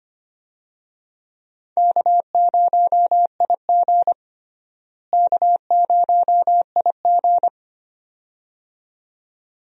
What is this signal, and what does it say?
Morse code, heard as a keyed tone, about 25 wpm, 700 Hz. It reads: K0IG K0IG
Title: USA Call Signs - 4 Characters - Repeated 2x - 25wpm